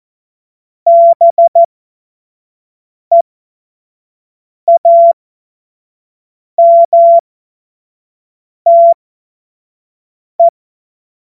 Morse code, heard as a keyed tone, about 14 wpm, spaced as for 5 wpm, 680 Hz. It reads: BEAMTE
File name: Witz-07-Oktober-2025-25-F70